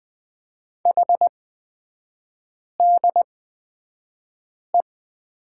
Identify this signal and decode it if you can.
Morse code, heard as a keyed tone, about 20 wpm, spaced as for 5 wpm, 700 Hz. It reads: HDE
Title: 255 Lesson 41e - Period - Three Characters 20wpm